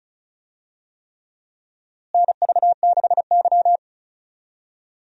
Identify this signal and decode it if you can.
Morse code, heard as a keyed tone, about 34 wpm, 700 Hz. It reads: NV6Y
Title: USA Call Signs - 4 Characters - Repeated 3x - 35wpm